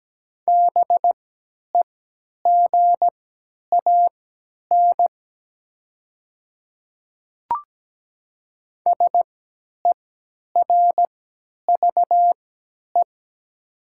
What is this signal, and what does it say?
Morse code, heard as a keyed tone, about 17 wpm, spaced as for 10 wpm, 700 Hz. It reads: BEGAN SERVE
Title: 3x Character Spacing - Top 500 Words - 17wpm